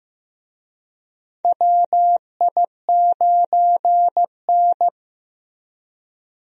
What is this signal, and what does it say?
Morse code, heard as a keyed tone, about 15 wpm, 700 Hz. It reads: WI9N
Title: USA Call Signs - 4 Characters - Repeated 3x - 15wpm